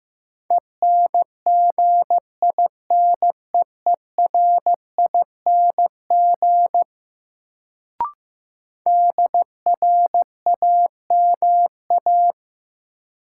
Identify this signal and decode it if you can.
Morse code, heard as a keyed tone, about 16 wpm, 700 Hz. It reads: ENGINEERING DRAMA
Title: Sentences from Top 1500 Words - New Word Drill Alt - 15wpm